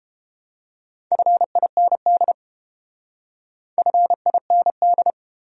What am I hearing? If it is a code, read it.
Morse code, heard as a keyed tone, about 32 wpm, 700 Hz. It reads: FIND FIND